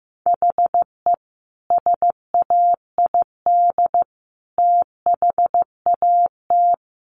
Morse code, heard as a keyed tone, about 15 wpm, 700 Hz. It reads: HE SAID THAT
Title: Sentences from Top 1400 Words 15wpm